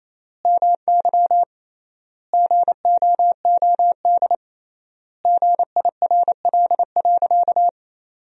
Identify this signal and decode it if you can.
Morse code, heard as a keyed tone, about 28 wpm, 700 Hz. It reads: MY GOOD GIRL.
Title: 3x Word-Spacing - Sentences from Top 400 Words - 28wpm